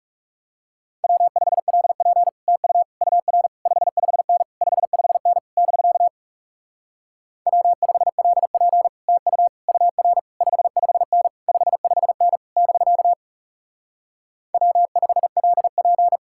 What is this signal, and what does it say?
Morse code, heard as a keyed tone, about 38 wpm, 700 Hz. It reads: W5LP TU UR 55N 55N <BK> W5LP TU UR 55N 55N <BK> W5LP
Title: Speed-Racing Common QSO Phrases - 30wpm